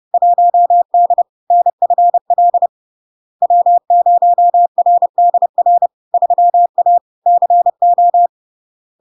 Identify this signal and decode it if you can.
Morse code, heard as a keyed tone, about 30 wpm, 700 Hz. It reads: 1D NFL W0RDR 3A CO
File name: LARK-FD-2022-Log-in-CW-30wpm-700-Hz